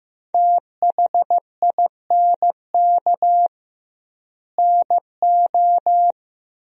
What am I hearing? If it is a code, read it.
Morse code, heard as a keyed tone, about 15 wpm, 700 Hz. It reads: THINK NO
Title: 2x Word-Spacing - Sentences from Top 200 Words - 15wpm